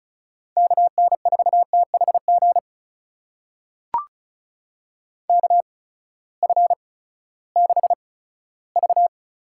Morse code, heard as a keyed tone, about 36 wpm, spaced as for 18 wpm, 700 Hz. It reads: KN4THG KF6V
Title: USA Call Signs - 6 Characters - 8x Character Spacing - 35wpm